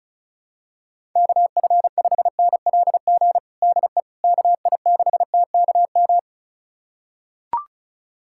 Morse code, heard as a keyed tone, about 35 wpm, 700 Hz. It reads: KF5NLG DE KI6TKM
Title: USA Callsign DE Callsign - 6 Characters - 35wpm